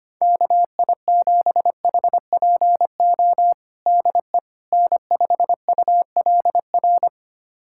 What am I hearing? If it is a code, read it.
Morse code, heard as a keyed tone, about 25 wpm, 700 Hz. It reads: KI7HPO DE N5ULR
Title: USA Callsign DE Callsign 25wpm